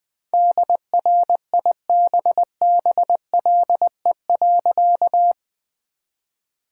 Morse code, heard as a keyed tone, about 20 wpm, 700 Hz. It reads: DRIBBLE.
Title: All-Age Appropriate Jokes - 20wpm